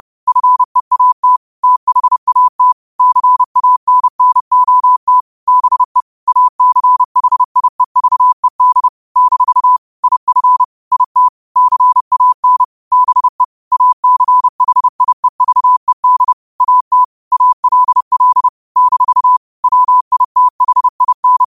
Morse code, heard as a keyed tone, about 30 wpm, 1000 Hz. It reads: FEAT THAT CANNOT BE ACHIEVED - IF IT CAN BE ACHIEVED AT ALL - WITHIN
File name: qotdfiles / 30wpm